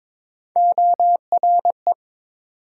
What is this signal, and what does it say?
Morse code, heard as a keyed tone, about 30 wpm, spaced as for 26 wpm, 700 Hz. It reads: ORE
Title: Speed-Racing Multi-letter Phonemes - 22wpm